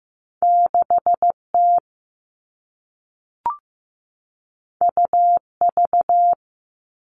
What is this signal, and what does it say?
Morse code, heard as a keyed tone, about 15 wpm, 700 Hz. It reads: 6T UV
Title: All Cut Numbers - 2-Digit 15wpm